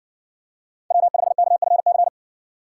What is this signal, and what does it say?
Morse code, heard as a keyed tone, about 60 wpm, 700 Hz. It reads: W5KFC